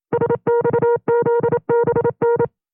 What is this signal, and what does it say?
Morse code, heard as a keyed tone, about 27 wpm, 470 Hz. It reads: SXZBN
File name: FreeFiles / CW-practice / 25WPM-side1